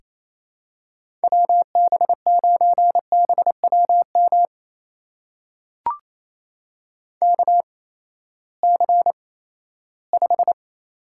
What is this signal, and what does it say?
Morse code, heard as a keyed tone, about 28 wpm, spaced as for 16 wpm, 700 Hz. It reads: WB9BWM KC5
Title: USA Call Signs - 6 Characters - 8x Character Spacing - 28wpm